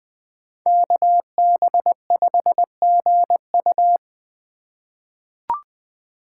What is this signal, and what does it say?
Morse code, heard as a keyed tone, about 20 wpm, 700 Hz. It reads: KB5GU
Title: USA Call Signs - 5 Characters - 4x Character Spacing - 20wpm